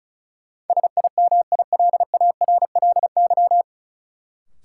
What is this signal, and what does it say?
Morse code, heard as a keyed tone, about 36 wpm, 700 Hz. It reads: SIMILARLY
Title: Sentences from Top 1400 Words - New Word Drill - 35wpm